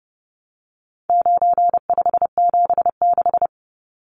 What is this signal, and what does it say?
Morse code, heard as a keyed tone, about 30 wpm, 700 Hz. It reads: 9576
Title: Numbers - 4-Digit 30wpm